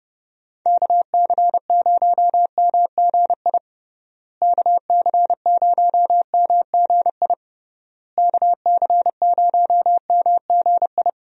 Morse code, heard as a keyed tone, about 31 wpm, 700 Hz. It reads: KC0MGI KC0MGI KC0MGI
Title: USA Call Signs - 6 Characters - Repeated 3x - 30wpm